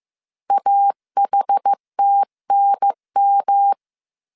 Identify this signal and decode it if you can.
Morse code, heard as a keyed tone, about 15 wpm, 800 Hz. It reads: AHTNM
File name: CW / DET / 1 3 1500C S C 14wpm 2458